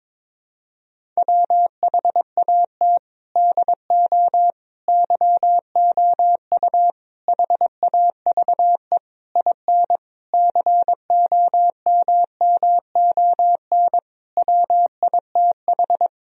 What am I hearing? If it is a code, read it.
Morse code, heard as a keyed tone, about 22 wpm, 700 Hz. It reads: WHAT DO YOU HAVE IN COMMON WITH
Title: Sentences from Top 1000 Words - Review Part 3 - 22wpm